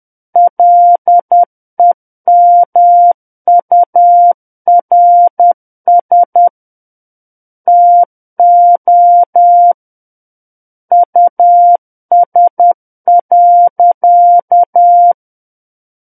Morse code, heard as a keyed tone, about 10 wpm, 700 Hz. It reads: LEMURS TO US.
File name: Book One / 01-Chapter 1 - 10 WPM